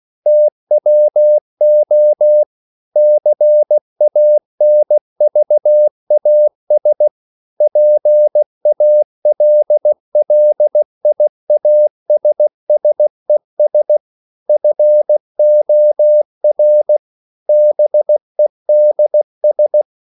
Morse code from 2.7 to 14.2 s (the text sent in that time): CANVAS PALLIASSES